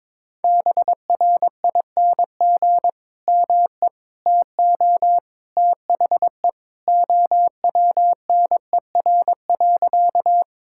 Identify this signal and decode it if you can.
Morse code, heard as a keyed tone, about 22 wpm, 700 Hz. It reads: BRING ME TO THE OWNER.
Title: Sentences from Top 1600 Words 22wpm